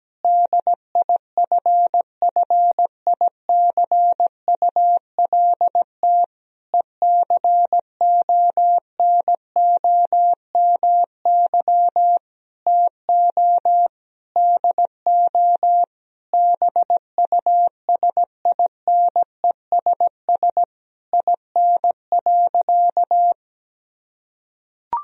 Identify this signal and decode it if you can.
Morse code, heard as a keyed tone, about 17 wpm, 700 Hz. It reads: DIFFICULT ECONOMY TO DO BUSINESS IN.
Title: Sentences from Top 1000 Words - Review Part 3 - 17wpm